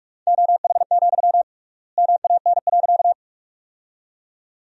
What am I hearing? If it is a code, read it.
Morse code, heard as a keyed tone, about 41 wpm, 700 Hz. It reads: OH, MAN.
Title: 3x Word-Spacing - Sentences from Top 500 Words - 45wpm